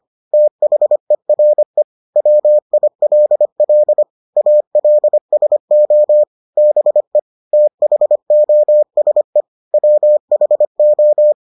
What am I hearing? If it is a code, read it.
Morse code, heard as a keyed tone, about 25 wpm, 600 Hz. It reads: THERE WILL ALSO BE THOSE WHO